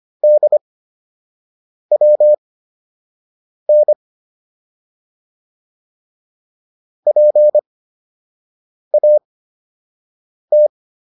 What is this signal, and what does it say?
Morse code, heard as a keyed tone, about 25 wpm, spaced as for 6 wpm, 600 Hz. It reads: DWN PAT